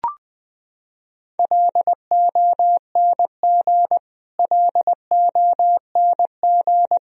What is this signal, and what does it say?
Morse code, heard as a keyed tone, about 20 wpm, 700 Hz. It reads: LONG LONG
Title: Top 100 Words Repeated 20wpm